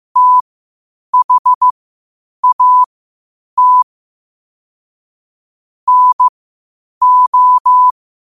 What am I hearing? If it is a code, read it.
Morse code, heard as a keyed tone, about 15 wpm, spaced as for 8 wpm, 1000 Hz. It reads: THAT NO
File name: qotdfiles / 8wpm